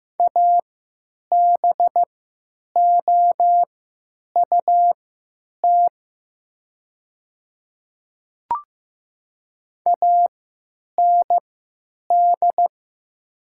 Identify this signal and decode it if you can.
Morse code, heard as a keyed tone, about 15 wpm, spaced as for 9 wpm, 700 Hz. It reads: ABOUT AND
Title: 3x Character Spacing - Top 200 Words - 15wpm